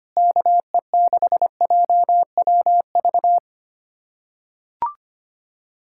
Morse code, heard as a keyed tone, about 25 wpm, 700 Hz. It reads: KE6JWV
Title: USA Callsign DE Callsign 25wpm